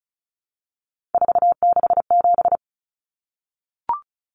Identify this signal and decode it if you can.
Morse code, heard as a keyed tone, about 34 wpm, 700 Hz. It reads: 467
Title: Numbers - 3-Digit 35wpm